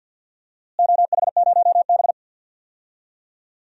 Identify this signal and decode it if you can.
Morse code, heard as a keyed tone, about 48 wpm, 700 Hz. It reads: OH0B
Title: DXCC-100 Call Signs - 4 Characters - Repeated 3x - 50wpm